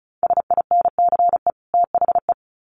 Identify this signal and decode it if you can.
Morse code, heard as a keyed tone, about 34 wpm, 700 Hz. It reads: SINCE THE
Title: Sets of 2 Words 35wpm